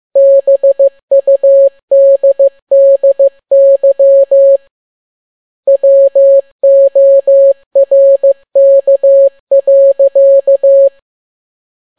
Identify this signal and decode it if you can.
Morse code, heard as a keyed tone, about 15 wpm, 550 Hz. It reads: BUDDY WORK.
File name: n6tt p1 15C13